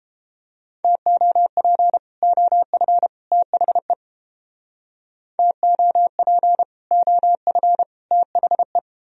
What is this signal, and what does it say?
Morse code, heard as a keyed tone, about 29 wpm, 700 Hz. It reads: TOP OF THE TOP OF THE